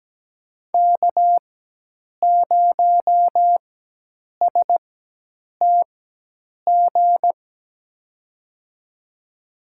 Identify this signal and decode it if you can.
Morse code, heard as a keyed tone, about 17 wpm, spaced as for 8 wpm, 700 Hz. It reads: K0STG
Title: USA Call Signs - 5 Characters - 4x Character Spacing - 17wpm